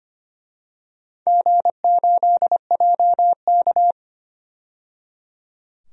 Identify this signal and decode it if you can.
Morse code, heard as a keyed tone, about 25 wpm, 700 Hz. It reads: G8JK
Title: DXCC-100 Call Signs - 4 Characters - 25wpm